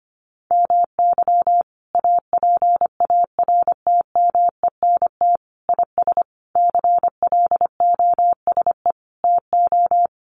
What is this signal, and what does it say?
Morse code, heard as a keyed tone, about 25 wpm, 700 Hz. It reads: MY APARTMENT IS CLOSE TO
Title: Sentences from Top 900 Words 25wpm